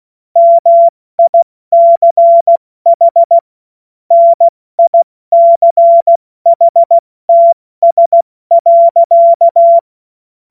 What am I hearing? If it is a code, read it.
Morse code, heard as a keyed tone, about 16 wpm, 680 Hz. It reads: MICH NICHTS.
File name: Morsekurs / Witze / Witz 19 März 2024 70 F80